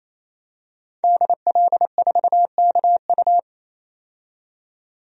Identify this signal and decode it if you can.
Morse code, heard as a keyed tone, about 27 wpm, 700 Hz. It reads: DL4KU
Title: DXCC-100 Call Signs - 5 Characters - 28wpm